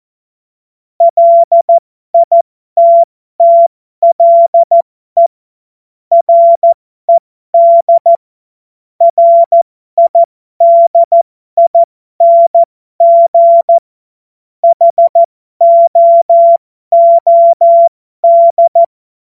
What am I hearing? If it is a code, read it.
Morse code, heard as a keyed tone, about 14 wpm, spaced as for 12 wpm, 680 Hz. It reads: LITTLE RED RIDING HOOD
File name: Morsekurs / Witze